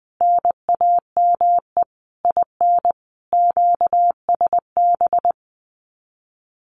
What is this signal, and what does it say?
Morse code, heard as a keyed tone, about 20 wpm, 700 Hz. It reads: NAME IN QSB
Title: Common QSO Phrases 20wpm